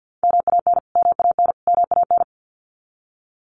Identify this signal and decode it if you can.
Morse code, heard as a keyed tone, about 50 wpm, 700 Hz. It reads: MAN MAN MAN
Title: Top 500 Words Repeated 50wpm